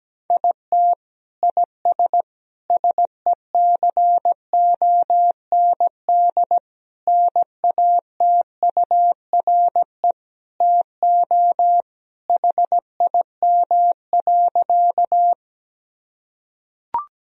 Morse code, heard as a keyed tone, about 17 wpm, 700 Hz. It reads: IT IS SECOND NATURE TO HIM.
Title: Sentences from Top 1000 Words - Review Part 1 - 17wpm